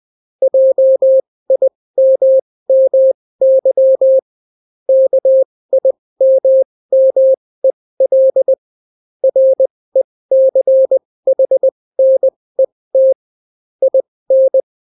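Morse code, heard as a keyed tone, about 20 wpm, spaced as for 16 wpm, 530 Hz. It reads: JIMMY KIMMEL RECHNET IN
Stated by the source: Daily Morse weather and top stories at 16 wpm, clean without QRN.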